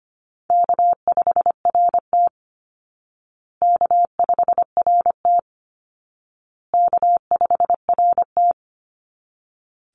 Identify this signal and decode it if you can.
Morse code, heard as a keyed tone, about 25 wpm, 700 Hz. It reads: K5RT K5RT K5RT
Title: Easy Call Signs Repeated 3x - 25wpm